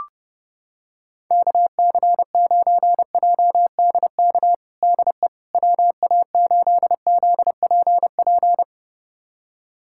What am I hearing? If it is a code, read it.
Morse code, heard as a keyed tone, about 30 wpm, 700 Hz. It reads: KC9JDK DE WA8ZPP
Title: USA Callsign DE Callsign - 6 Characters - 30wpm